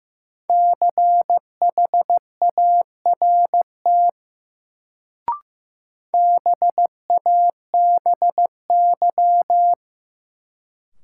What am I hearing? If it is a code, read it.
Morse code, heard as a keyed tone, about 15 wpm, 700 Hz. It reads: CHART BABY
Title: Sentences from Top 1300 Words - New Word Drill - 15wpm